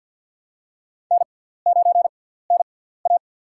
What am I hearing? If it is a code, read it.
Morse code, heard as a keyed tone, about 53 wpm, spaced as for 17 wpm, 700 Hz. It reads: N9NA
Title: USA Call Signs - 4 Characters - 6x Character Spacing - 50wpm